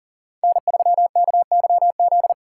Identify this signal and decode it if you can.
Morse code, heard as a keyed tone, about 40 wpm, 700 Hz. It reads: N3KYZ
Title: USA Call Signs - 5 Characters - 4x Character Spacing - 40wpm